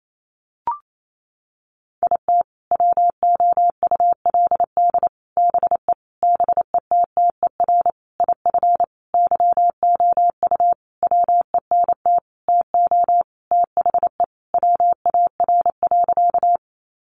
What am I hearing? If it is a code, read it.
Morse code, heard as a keyed tone, about 28 wpm, 700 Hz. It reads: IT WOULD BE BETTER IF YOU WENT TO THE WAR.
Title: Sentences from Top 400 Words 28wpm